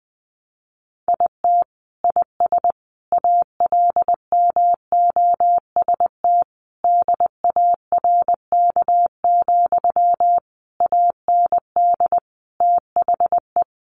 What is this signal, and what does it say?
Morse code, heard as a keyed tone, about 20 wpm, 700 Hz. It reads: IT IS ALMOST DARK, AND THE
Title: Sentences from Top 1200 Words 20wpm